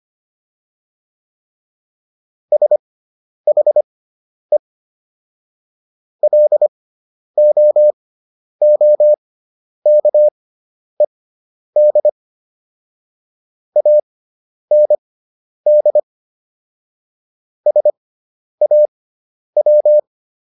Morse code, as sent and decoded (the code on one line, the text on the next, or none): ... .... / .-.. --- --- -.- . -.. / .- -. -.. / ... .- .--
SHE LOOKED AND SAW